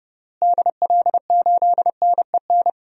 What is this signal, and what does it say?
Morse code, heard as a keyed tone, about 30 wpm, 700 Hz. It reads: DL8NEN
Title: DXCC-100 Call Signs - 6 Characters - Repeated 3x - 30wpm